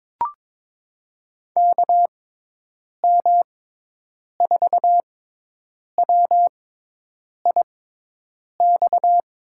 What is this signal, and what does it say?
Morse code, heard as a keyed tone, about 22 wpm, spaced as for 8 wpm, 700 Hz. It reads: KM4WIX
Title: USA Call Signs - 6x Character Spacing - 22wpm